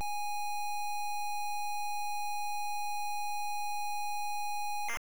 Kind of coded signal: ZX Spectrum tape loading